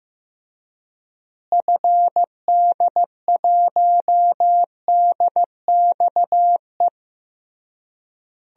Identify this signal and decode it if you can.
Morse code, heard as a keyed tone, about 15 wpm, 700 Hz. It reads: FD1DXE
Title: DXCC-100 Call Signs - 6 Characters - Repeated 3x - 15wpm